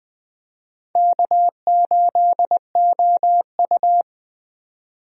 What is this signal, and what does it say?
Morse code, heard as a keyed tone, about 20 wpm, 700 Hz. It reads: K8OU
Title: USA Call Signs - 4 Characters - 20wpm